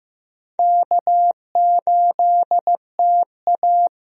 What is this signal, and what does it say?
Morse code, heard as a keyed tone, about 15 wpm, 700 Hz. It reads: K8TA
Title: USA Call Signs - 4 Characters - Repeated 2x - 15wpm